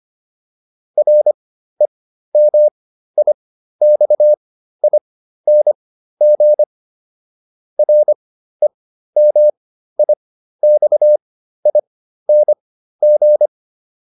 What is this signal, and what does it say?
Morse code, heard as a keyed tone, about 25 wpm, spaced as for 13 wpm, 600 Hz. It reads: REMIXING REMIXING
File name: RE8 13